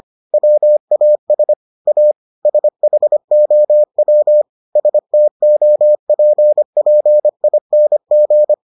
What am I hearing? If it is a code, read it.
Morse code, heard as a keyed tone, about 25 wpm, 600 Hz. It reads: WAS A SHOW STOPPING